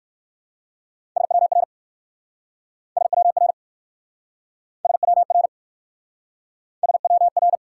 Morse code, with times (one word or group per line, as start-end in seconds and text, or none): 1.2-1.6 SWR
3.0-3.5 SWR
4.9-5.5 SWR
6.8-7.6 SWR